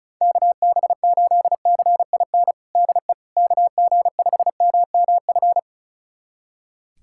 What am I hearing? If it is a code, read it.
Morse code, heard as a keyed tone, about 36 wpm, 700 Hz. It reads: KB8CIN DE KG5MMF
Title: USA Callsign DE Callsign - 6 Characters - 35wpm